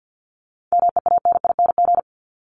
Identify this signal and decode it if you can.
Morse code, heard as a keyed tone, about 50 wpm, 700 Hz. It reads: MEANING
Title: ING Suffix - Top 100 Words with Base Words 50wpm